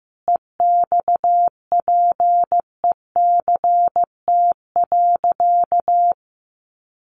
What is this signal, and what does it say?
Morse code, heard as a keyed tone, about 15 wpm, 700 Hz. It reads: EXPECT.
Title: Sentences from Top 1100 Words 15wpm